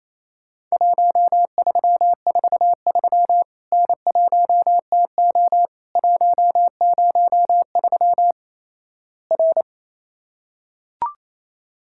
Morse code, heard as a keyed tone, about 27 wpm, 700 Hz. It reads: 1343 N1TO 103 R
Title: Worked All Europe DX Contest - QTC Exchange - 28wpm